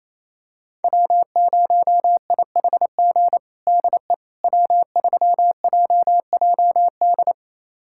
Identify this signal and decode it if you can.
Morse code, heard as a keyed tone, about 28 wpm, 700 Hz. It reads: W0IHG DE W3JJD
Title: USA Callsign DE Callsign - 5 Characters - 28wpm